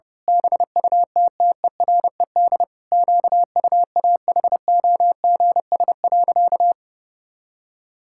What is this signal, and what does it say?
Morse code, heard as a keyed tone, about 32 wpm, 700 Hz. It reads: BUTTERED QUAHOGS.
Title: Pangram Challenge v2 - 30wpm